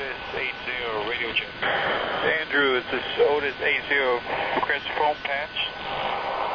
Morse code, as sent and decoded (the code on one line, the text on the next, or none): none